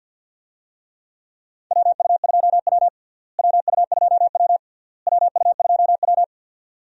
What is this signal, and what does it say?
Morse code, heard as a keyed tone, about 46 wpm, 700 Hz. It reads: WU2W WU2W WU2W
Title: USA Call Signs - 4 Characters - Repeated 3x - 50wpm